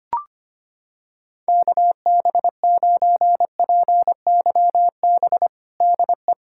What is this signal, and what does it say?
Morse code, heard as a keyed tone, about 25 wpm, 700 Hz. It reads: KB9PYB DE
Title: USA Callsign DE Callsign 25wpm